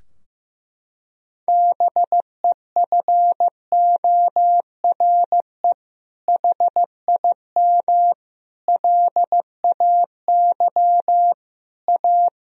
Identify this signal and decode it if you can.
Morse code, heard as a keyed tone, about 15 wpm, 700 Hz. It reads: BEFORE HIM LAY A
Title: Sentences from Top 1000 Words - Review Part 2 - 15wpm